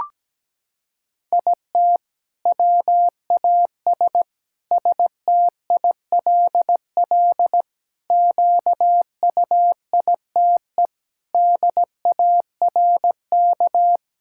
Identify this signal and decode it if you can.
Morse code, heard as a keyed tone, about 17 wpm, 700 Hz. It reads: IT WAS STILL QUITE DARK
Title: Sentences from Top 1000 Words - Review Part 6 - 17wpm